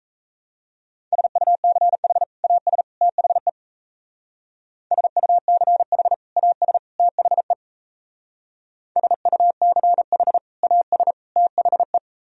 Morse code, heard as a keyed tone, about 35 wpm, 700 Hz. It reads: SUCH AS THE SUCH AS THE SUCH AS THE